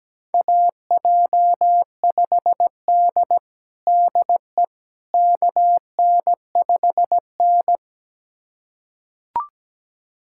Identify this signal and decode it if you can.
Morse code, heard as a keyed tone, about 17 wpm, 700 Hz. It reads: AJ5D DE KN5N
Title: USA Callsign DE Callsign - 4 Characters - 17wpm